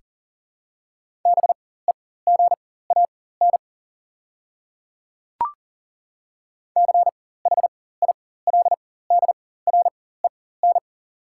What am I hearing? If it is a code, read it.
Morse code, heard as a keyed tone, about 40 wpm, spaced as for 19 wpm, 700 Hz. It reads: BEGAN CHILDREN
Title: 4x Character Spacing - Top 300 Words - 40wpm